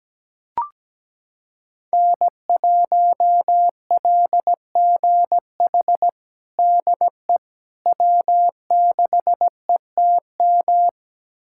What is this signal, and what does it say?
Morse code, heard as a keyed tone, about 17 wpm, 700 Hz. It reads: N1LGH DE W6ETM
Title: USA Callsign DE Callsign - 5 Characters - 17wpm